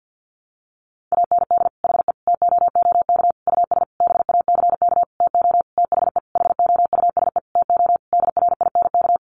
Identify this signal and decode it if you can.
Morse code, heard as a keyed tone, about 50 wpm, 700 Hz. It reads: AND HE TOOK US BACK TO THE HOUSE TO DRINK
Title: Sentences from Top 1500 Words 50wpm